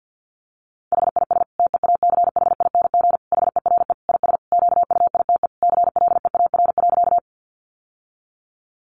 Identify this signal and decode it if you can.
Morse code, heard as a keyed tone, about 50 wpm, 700 Hz. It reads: HIS TEACHING HERE IS QUITE CLEAR.
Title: Sentences from Top 1100 Words 50wpm